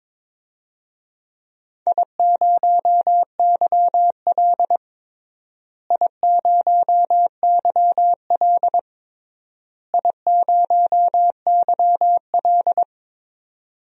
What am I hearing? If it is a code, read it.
Morse code, heard as a keyed tone, about 22 wpm, 700 Hz. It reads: I0YL I0YL I0YL
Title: DXCC-100 Call Signs - 4 Characters - Repeated 3x - 22wpm